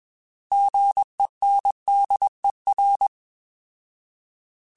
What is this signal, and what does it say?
Morse code, heard as a keyed tone, about 21 wpm, 800 Hz. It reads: GENDER
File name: Alphabetical / Alphabetical mp3s / Gs @20